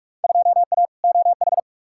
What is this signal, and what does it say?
Morse code, heard as a keyed tone, about 46 wpm, 700 Hz. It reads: 2A OH
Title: Field Day Exchanges - 45wpm